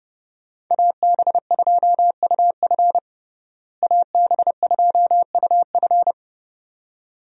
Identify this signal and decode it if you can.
Morse code, heard as a keyed tone, about 32 wpm, 700 Hz. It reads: AB2UF AB2UF
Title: USA Call Signs - 5 Characters - Repeated 2x - 30wpm